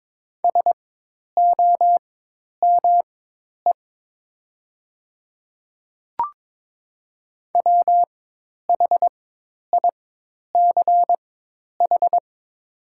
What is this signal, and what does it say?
Morse code, heard as a keyed tone, about 23 wpm, spaced as for 10 wpm, 700 Hz. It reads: SOME WHICH
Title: 4x Character Spacing - Top 100 Words - 22wpm